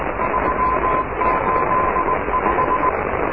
Morse code, heard as a keyed tone, about 20 wpm, 1000 Hz, with noise. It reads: 385
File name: oscar 7 de k5dze section 2